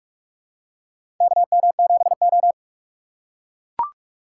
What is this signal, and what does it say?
Morse code, heard as a keyed tone, about 43 wpm, 700 Hz. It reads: KM7O
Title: USA Call Signs - 4 Characters - 6x Character Spacing - 45wpm